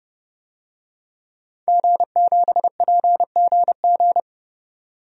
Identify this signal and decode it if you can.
Morse code, heard as a keyed tone, about 30 wpm, 700 Hz. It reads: G7PGG
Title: DXCC-100 Call Signs - 5 Characters - 30wpm